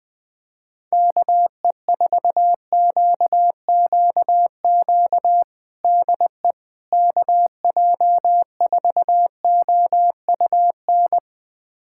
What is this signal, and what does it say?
Morse code, heard as a keyed tone, about 21 wpm, 700 Hz. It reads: KE4QQQ DE KJ4OUN
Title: USA Callsign DE Callsign - 6 Characters - 20wpm